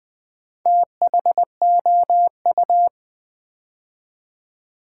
Morse code, heard as a keyed tone, about 20 wpm, 700 Hz. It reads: THOU
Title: Build A Word - 2 Letters At A Time - 8-Letter Words - 20wpm